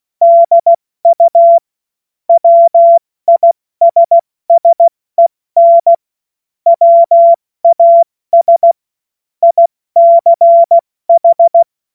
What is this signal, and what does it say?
Morse code, heard as a keyed tone, about 16 wpm, 680 Hz. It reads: DU WISSEN WAS ICH
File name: Morsekurs / Witze / Witz 16. November 2021, 70 F80